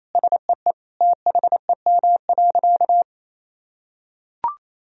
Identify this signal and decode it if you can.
Morse code, heard as a keyed tone, about 28 wpm, 700 Hz. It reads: SEE THEM.
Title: Sentences from Top 1000 Words - Review Part 1 - 28wpm